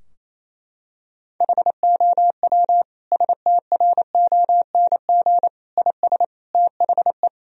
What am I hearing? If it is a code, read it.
Morse code, heard as a keyed tone, about 27 wpm, 700 Hz. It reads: HOW STRONG IS THE
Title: Sentences from Top 1000 Words - Review Part 4 - 28wpm